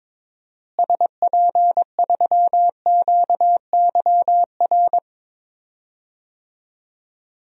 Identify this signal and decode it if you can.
Morse code, heard as a keyed tone, about 22 wpm, 700 Hz. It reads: SP3QYR
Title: DXCC-100 Call Signs - 6 Characters - Repeated 3x - 22wpm